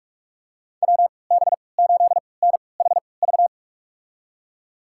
Morse code, heard as a keyed tone, about 47 wpm, spaced as for 26 wpm, 700 Hz. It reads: WB8NHV
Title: USA Call Signs - 6 Characters - 3x Character Spacing - 45wpm